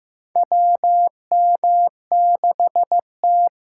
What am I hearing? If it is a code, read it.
Morse code, heard as a keyed tone, about 15 wpm, 700 Hz. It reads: WM6T
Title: USA Call Signs - 6x Character Spacing - 15wpm